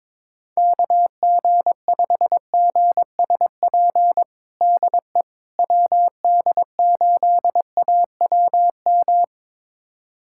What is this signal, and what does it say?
Morse code, heard as a keyed tone, about 22 wpm, 700 Hz. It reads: KG5GSP DE WD8AWM
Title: USA Callsign DE Callsign 22wpm